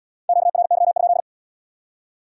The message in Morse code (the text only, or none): QRY?